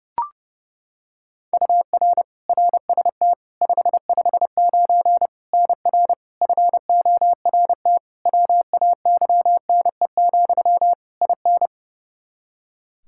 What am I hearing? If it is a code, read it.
Morse code, heard as a keyed tone, about 32 wpm, 700 Hz. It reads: UR RST 559 NR FORT WAYNE, IN
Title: Practice QSO - Exchange 1 - Beginning - 30wpm